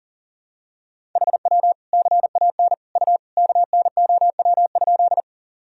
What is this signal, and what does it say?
Morse code, heard as a keyed tone, about 37 wpm, 700 Hz. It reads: HW CAN U KNOW?
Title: Sentences from Top 100 Words and Abbreviations 40wpm